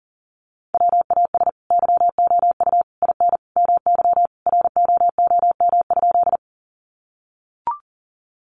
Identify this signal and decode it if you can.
Morse code, heard as a keyed tone, about 40 wpm, 700 Hz. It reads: WAS YOU IN MY ROOM?
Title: Sentences from Top 300 Words 40wpm